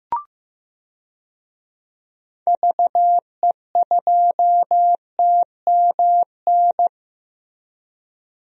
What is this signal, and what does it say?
Morse code, heard as a keyed tone, about 15 wpm, 700 Hz. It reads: VE2TMN
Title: DXCC-100 Call Signs - 6 Characters - Repeated 2x - 15wpm